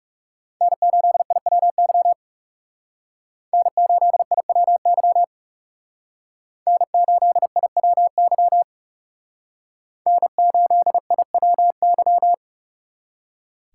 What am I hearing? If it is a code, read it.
Morse code, heard as a keyed tone, about 38 wpm, 700 Hz. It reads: N8IWY N8IWY N8IWY N8IWY